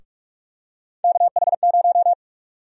Morse code, heard as a keyed tone, about 42 wpm, 700 Hz. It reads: KH0
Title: USA Call Sign Prefixes - 45wpm